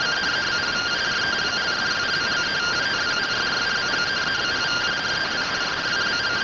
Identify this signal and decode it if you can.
Two-tone teleprinter signal, about 170 Hz shift.